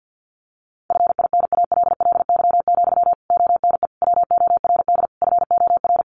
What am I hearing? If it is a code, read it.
Morse code, heard as a keyed tone, about 50 wpm, 700 Hz. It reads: FINALLY, ONE WORD FOR
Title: Sentences from Top 1300 Words 50wpm